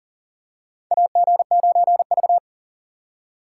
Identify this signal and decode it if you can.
Morse code, heard as a keyed tone, about 40 wpm, 700 Hz. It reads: AG9V
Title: USA Call Signs - 2x Character Spacing - 40wpm